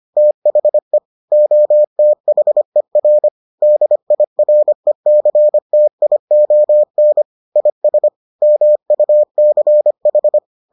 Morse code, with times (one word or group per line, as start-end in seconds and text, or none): 0.2-1.0 THE
1.3-3.3 OTHER
3.6-7.2 DIRECTION
7.6-8.1 IS
8.4-10.4 MUCH